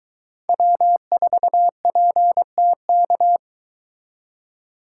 Morse code, heard as a keyed tone, about 22 wpm, 700 Hz. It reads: W4PTK